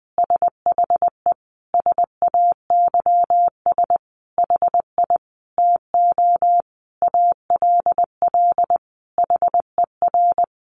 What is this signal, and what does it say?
Morse code, heard as a keyed tone, about 19 wpm, 700 Hz. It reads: SHE SAYS HI TO ALL HER
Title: Easy Phrases 20wpm